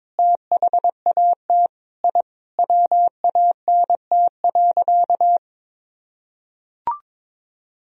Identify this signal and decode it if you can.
Morse code, heard as a keyed tone, about 22 wpm, 690 Hz. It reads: THAT I WANT.
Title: Sentences from Top 1000 Words - Review Part 3 - 22wpm